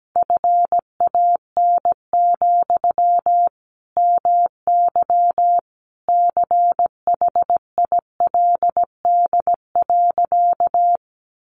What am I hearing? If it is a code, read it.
Morse code, heard as a keyed tone, about 17 wpm, 700 Hz. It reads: FAN, MY CHILD.
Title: Sentences from Top 1300 Words 17wpm